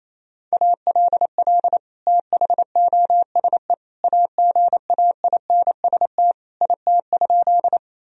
Morse code, heard as a keyed tone, about 28 wpm, 700 Hz. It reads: ALL THOSE AGAINST IT?
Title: Sentences from Top 1000 Words - Review Part 1 - 28wpm